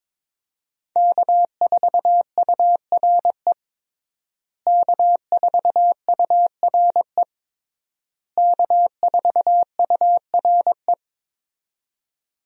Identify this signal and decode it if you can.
Morse code, heard as a keyed tone, about 22 wpm, 700 Hz. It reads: K4URE K4URE K4URE
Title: USA Call Signs - 5 Characters - Repeated 3x - 22wpm